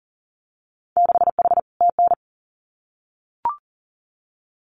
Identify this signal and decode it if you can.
Morse code, heard as a keyed tone, about 40 wpm, 700 Hz. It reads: 6H TN
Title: Winter Field Day Exchanges Encore 40wpm